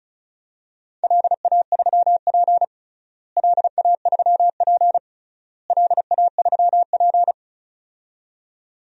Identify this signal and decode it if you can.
Morse code, heard as a keyed tone, about 34 wpm, 700 Hz. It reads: LA3P LA3P LA3P
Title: DXCC-100 Call Signs - 4 Characters - Repeated 3x - 35wpm